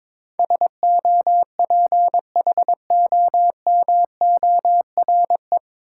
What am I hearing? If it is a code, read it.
Morse code, heard as a keyed tone, about 22 wpm, 700 Hz. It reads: SOPHOMORE
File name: 9 Letter Words - First 3 Letters Given - 22wpm